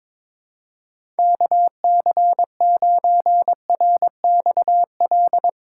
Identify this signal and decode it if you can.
Morse code, heard as a keyed tone, about 22 wpm, 700 Hz. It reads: KC9RXL
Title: USA Call Signs - 6 Characters - 2x Character Spacing - 22wpm